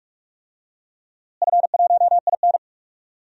Morse code, heard as a keyed tone, about 41 wpm, 700 Hz. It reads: F1IN
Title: DXCC-100 Call Signs - 4 Characters - Repeated 2x - 45wpm